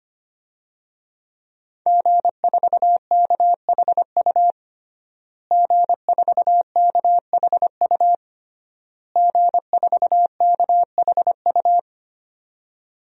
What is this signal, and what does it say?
Morse code, heard as a keyed tone, about 25 wpm, 700 Hz. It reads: G4KHU G4KHU G4KHU
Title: DXCC-100 Call Signs - 5 Characters - Repeated 3x - 25wpm